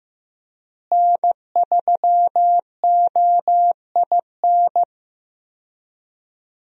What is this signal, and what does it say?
Morse code, heard as a keyed tone, about 15 wpm, 700 Hz. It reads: N3OIN
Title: USA Call Signs - 5 Characters - Repeated 3x - 15wpm